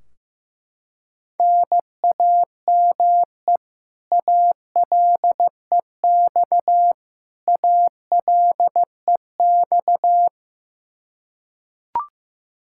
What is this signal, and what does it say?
Morse code, heard as a keyed tone, about 16 wpm, 700 Hz. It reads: NAME ALEX ALEX
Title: Practice QSO - Exchange 1 - Beginning - 15wpm